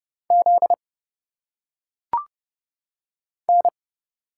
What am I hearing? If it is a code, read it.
Morse code, heard as a keyed tone, about 29 wpm, 700 Hz. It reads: Z N
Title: Single Letters - Phonetic Pronunciation 30wpm